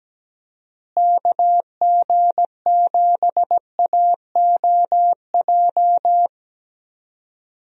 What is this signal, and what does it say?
Morse code, heard as a keyed tone, about 17 wpm, 700 Hz. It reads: KG7AOJ
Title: USA Call Signs - 6 Characters - Repeated 3x - 17wpm